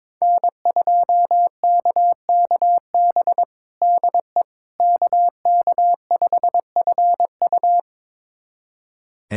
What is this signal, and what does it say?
Morse code, heard as a keyed tone, about 22 wpm, 700 Hz. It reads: N2KKB DE KK5FU
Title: USA Callsign DE Callsign 22wpm